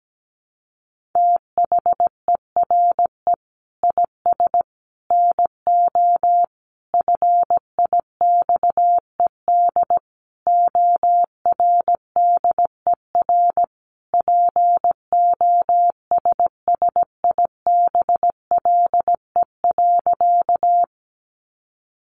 Morse code, heard as a keyed tone, about 17 wpm, 700 Hz. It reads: THERE IS NO FIXED ORDER POSSIBLE.
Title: Sentences from Top 1100 Words 17wpm